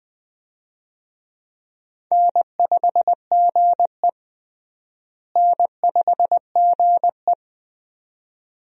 Morse code, heard as a keyed tone, about 20 wpm, 700 Hz. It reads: N5GE N5GE
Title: USA Call Signs - 4 Characters - Repeated 2x - 20wpm